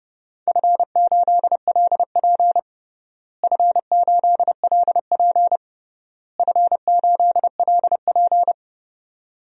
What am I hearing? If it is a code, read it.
Morse code, heard as a keyed tone, about 32 wpm, 700 Hz. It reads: F8LP F8LP F8LP
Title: DXCC-100 Call Signs - 4 Characters - Repeated 3x - 30wpm